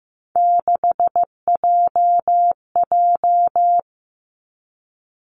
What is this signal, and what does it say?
Morse code, heard as a keyed tone, about 15 wpm, 700 Hz. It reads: 6JJ
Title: Three Challenging Letter-Number Combinations 15wpm